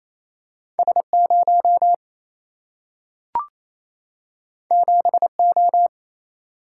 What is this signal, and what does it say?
Morse code, heard as a keyed tone, about 28 wpm, 700 Hz. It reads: S 0 7 O
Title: Single-Letter Single-Number 28wpm